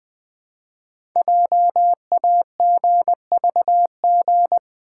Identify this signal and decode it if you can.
Morse code, heard as a keyed tone, about 20 wpm, 700 Hz. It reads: JAGVG
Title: Code Groups - 5 Letters - 20wpm